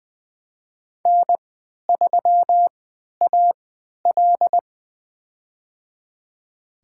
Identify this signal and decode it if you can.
Morse code, heard as a keyed tone, about 20 wpm, spaced as for 11 wpm, 700 Hz. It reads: N3AL
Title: USA Call Signs - 3x Character Spacing - 20wpm